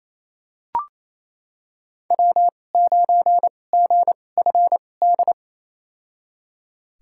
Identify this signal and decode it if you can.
Morse code, heard as a keyed tone, about 28 wpm, spaced as for 20 wpm, 700 Hz. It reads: W9GFD
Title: USA Call Signs - 5 Characters - 2x Character Spacing - 28wpm